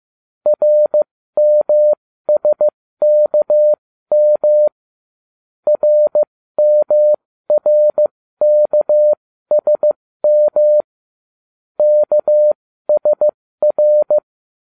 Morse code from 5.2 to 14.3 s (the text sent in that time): RMRKSM KSR